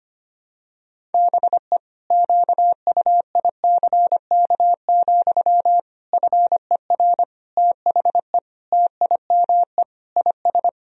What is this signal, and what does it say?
Morse code, heard as a keyed tone, about 25 wpm, 700 Hz. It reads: BE QUICK, FER THE TIME IS
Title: Sentences from Top 500 Words and Abbreviations 25wpm